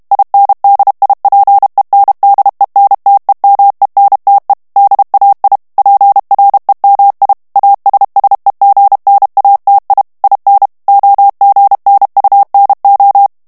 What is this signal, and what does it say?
Morse code, heard as a keyed tone, about 32 wpm, 800 Hz. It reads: INDIPENDENTEMENTE DAI PREMI ASSEGNATI IN OGNUNO
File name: ITALIEN texte clair 2 vitesse 30 mots minute